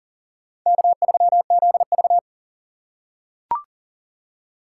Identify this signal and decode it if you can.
Morse code, heard as a keyed tone, about 42 wpm, 690 Hz. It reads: K3ZV
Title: USA Call Signs - 4 Characters - 2x Character Spacing - 40wpm